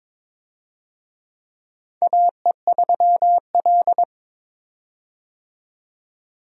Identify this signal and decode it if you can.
Morse code, heard as a keyed tone, about 22 wpm, 700 Hz. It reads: AE3L
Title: USA Call Signs - 4 Characters - 22wpm